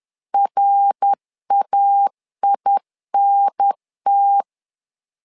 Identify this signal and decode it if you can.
Morse code, heard as a keyed tone, about 10 wpm, 800 Hz. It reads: RAINT
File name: LEZIONI / 3 MWRO / 3 MWR 10wpm 1952